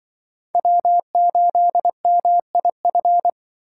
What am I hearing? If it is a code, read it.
Morse code, heard as a keyed tone, about 24 wpm, 700 Hz. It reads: W8MIF
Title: Speed Racing - USA 1x3 Call Signs 20wpm